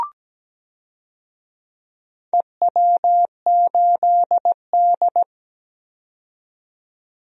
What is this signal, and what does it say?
Morse code, heard as a keyed tone, about 17 wpm, 700 Hz. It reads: EW8D
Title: DXCC-100 Call Signs - 4 Characters - 17wpm